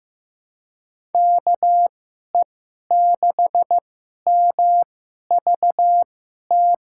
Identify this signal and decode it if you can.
Morse code, heard as a keyed tone, about 15 wpm, spaced as for 11 wpm, 700 Hz. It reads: KE6MVT
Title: USA Call Signs - 6 Characters - 2x Character Spacing - 15wpm